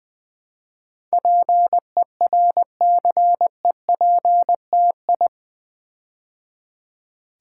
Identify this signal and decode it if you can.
Morse code, heard as a keyed tone, about 20 wpm, 700 Hz. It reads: PERCEPTI
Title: Build A Word - 4 Letters At A Time - 10-Letter Words - 20wpm